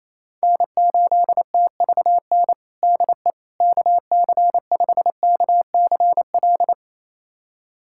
Morse code, heard as a keyed tone, about 28 wpm, 700 Hz. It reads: N8TVN DE KC5KCL
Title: USA Callsign DE Callsign 28wpm